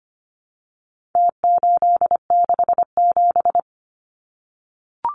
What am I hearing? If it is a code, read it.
Morse code, heard as a keyed tone, about 25 wpm, 700 Hz. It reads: T867
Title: Numbers - Contest Style - 4-Digit 25wpm